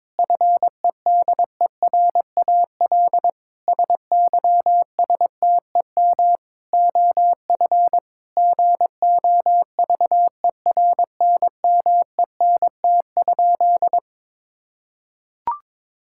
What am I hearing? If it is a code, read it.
Morse code, heard as a keyed tone, about 22 wpm, 700 Hz. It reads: FEDERAL SYSTEM OF GOVERNMENT?
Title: Sentences from Top 1000 Words - Review Part 5 - 22wpm